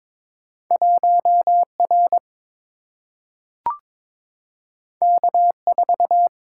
Morse code, heard as a keyed tone, about 22 wpm, 700 Hz. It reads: 1R K4
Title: Single-Letter Single-Number 22wpm